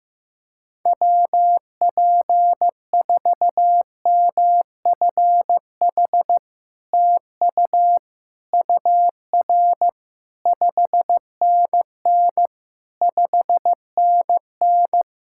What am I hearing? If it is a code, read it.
Morse code, heard as a keyed tone, about 15 wpm, 690 Hz. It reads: WP4MFH TU UR 5NN 5NN
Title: SOTA QSO Practice 15wpm